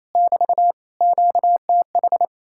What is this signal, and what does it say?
Morse code, heard as a keyed tone, about 33 wpm, 700 Hz. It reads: <BT> QTH
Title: Speed-Racing Common QSO Phrases - 28wpm